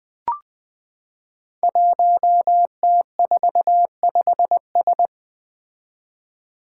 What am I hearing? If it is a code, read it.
Morse code, heard as a keyed tone, about 20 wpm, 700 Hz. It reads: 1T45S
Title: Five Challenging Letter-Number Combinations 20wpm